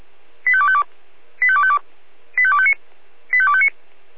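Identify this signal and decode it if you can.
Five-tone selective call (ZVEI), heard as quick runs of five tones, tones of about 70 ms. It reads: 85252 85252 85259 85259
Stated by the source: Five-tone ZVEI selective call 85253 received by SDR.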